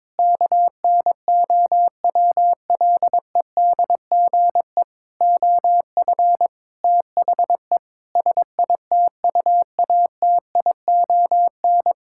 Morse code, heard as a keyed tone, about 21 wpm, 700 Hz. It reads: KNOWLEDGE OF THE SITUATION
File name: Sentences with 9-Word Phrases - 22wpm